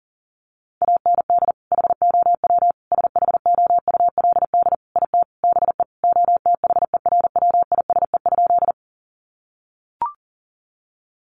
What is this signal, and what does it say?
Morse code, heard as a keyed tone, about 40 wpm, 700 Hz. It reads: AND HOW SHOULD IT BE OTHERWISE?
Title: Sentences from Top 1200 Words 40wpm